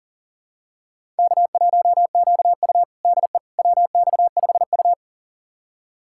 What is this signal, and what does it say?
Morse code, heard as a keyed tone, about 37 wpm, 700 Hz. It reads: K1QU DE WX5U
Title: USA Callsign DE Callsign - 4 Characters - 40wpm